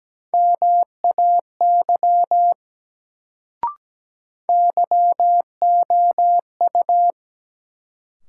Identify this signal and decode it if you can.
Morse code, heard as a keyed tone, about 17 wpm, 700 Hz. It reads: MAY YOU
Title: Sentences from Top 100 Words - New Word Drill - 17wpm